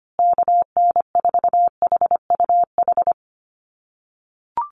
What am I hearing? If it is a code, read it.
Morse code, heard as a keyed tone, about 25 wpm, 700 Hz. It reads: KN4HUH
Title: USA Call Signs Repeated 3x 25wpm